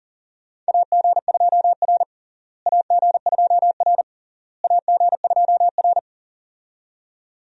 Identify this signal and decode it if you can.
Morse code, heard as a keyed tone, about 40 wpm, 700 Hz. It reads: AG2R AG2R AG2R
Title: USA Call Signs - 4 Characters - Repeated 3x - 40wpm